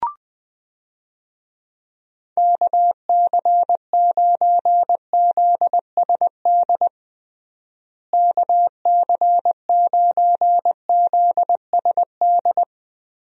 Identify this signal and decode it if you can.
Morse code, heard as a keyed tone, about 20 wpm, 700 Hz. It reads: KC9ZSD KC9ZSD
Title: USA Call Signs - 6 Characters - Repeated 3x - 20wpm